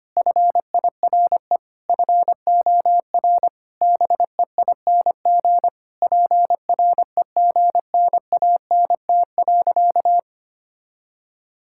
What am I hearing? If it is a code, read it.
Morse code, heard as a keyed tone, about 25 wpm, 700 Hz. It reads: FIRE FOR BEING PREGNANT.
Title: Sentences from Top 1000 Words - Review Part 6 - 25wpm